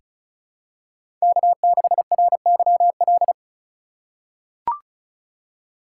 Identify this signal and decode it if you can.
Morse code, heard as a keyed tone, about 34 wpm, 700 Hz. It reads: K6RYL
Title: USA Call Signs - 5 Characters - 2x Character Spacing - 35wpm